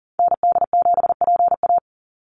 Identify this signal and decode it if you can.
Morse code, heard as a keyed tone, about 40 wpm, 700 Hz. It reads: ND7PA
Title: NA SOTA Activator Callsigns 40wpm